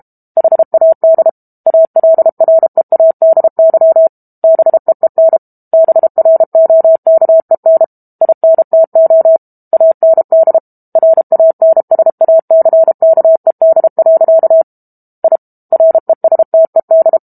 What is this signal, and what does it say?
Morse code, heard as a keyed tone, about 32 wpm, 650 Hz. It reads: HAD ALREADY BEEN BROKEN INTO AND RANSACKED. I RESTED
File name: Book Two / 06-Chapter 6 - 32 WPM